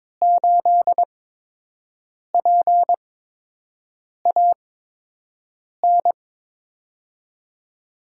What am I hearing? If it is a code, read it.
Morse code, heard as a keyed tone, about 22 wpm, spaced as for 6 wpm, 700 Hz. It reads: 8PAN
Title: USA Call Signs - 5 Characters - 8x Character Spacing - 22wpm